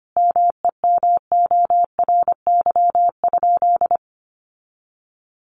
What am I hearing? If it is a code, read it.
Morse code, heard as a keyed tone, about 25 wpm, 700 Hz. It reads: MEMORY?
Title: Sentences from Top 1300 Words 25wpm